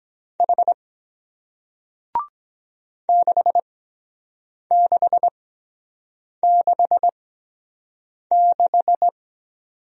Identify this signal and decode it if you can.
Morse code, heard as a keyed tone, about 23 wpm, spaced as for 7 wpm, 700 Hz. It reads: H 6666